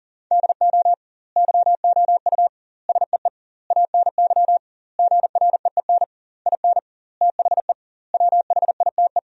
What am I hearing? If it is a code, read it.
Morse code, heard as a keyed tone, about 37 wpm, 700 Hz. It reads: DO YOU SEE ANY GREEN IN THE WHITE
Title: 2x Word-Spacing - Sentences from Top 500 Words - 40wpm